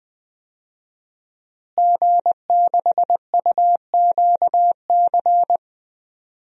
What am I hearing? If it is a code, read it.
Morse code, heard as a keyed tone, about 21 wpm, 700 Hz. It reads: G6UQC
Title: DXCC-100 Call Signs - 5 Characters - Repeated 2x - 20wpm